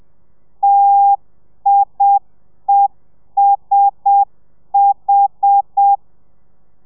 Morse code, heard as a keyed tone, about 7 wpm, 800 Hz. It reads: TIESH
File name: telegrafia / i2tfj / Lez1 30